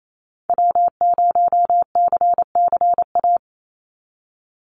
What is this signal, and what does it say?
Morse code, heard as a keyed tone, about 28 wpm, 700 Hz. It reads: W0CCA
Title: NA SOTA Activator Callsigns 28wpm